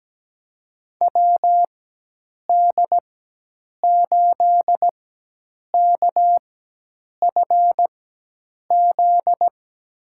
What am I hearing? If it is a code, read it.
Morse code, heard as a keyed tone, about 17 wpm, spaced as for 8 wpm, 700 Hz. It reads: WD8KFZ
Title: USA Call Signs - 6 Characters - 4x Character Spacing - 17wpm